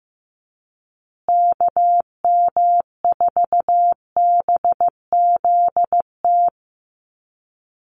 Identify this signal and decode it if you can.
Morse code, heard as a keyed tone, about 15 wpm, 700 Hz. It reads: KM4BZT
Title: USA Call Signs Repeated 2x 15wpm